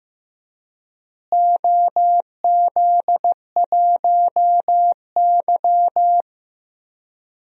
Morse code, heard as a keyed tone, about 15 wpm, 700 Hz. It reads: OZ1Y
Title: DXCC-100 Call Signs - 4 Characters - Repeated 3x - 15wpm